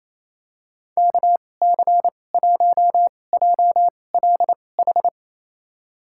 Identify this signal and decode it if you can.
Morse code, heard as a keyed tone, about 28 wpm, spaced as for 20 wpm, 700 Hz. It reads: KC1JLH
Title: USA Call Signs - 2x Character Spacing - 28wpm